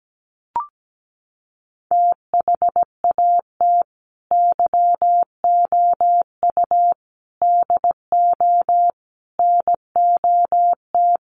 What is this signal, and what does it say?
Morse code, heard as a keyed tone, about 17 wpm, 700 Hz. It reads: THAT YOU DO NOT
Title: Sets of 4 Words 17wpm